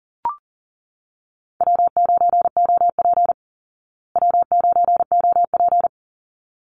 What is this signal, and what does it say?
Morse code, heard as a keyed tone, about 40 wpm, 700 Hz. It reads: W9OP W9OP
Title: Call Signs Repeated 3x - 40wpm